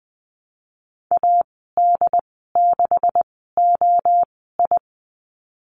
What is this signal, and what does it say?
Morse code, heard as a keyed tone, about 20 wpm, spaced as for 15 wpm, 700 Hz. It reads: AD6OI
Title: 140 Lesson 23g - Letter f - Call Signs 20wpm